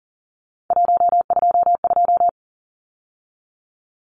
Morse code, heard as a keyed tone, about 37 wpm, 700 Hz. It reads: 122
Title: Numbers - 3-Digit 40wpm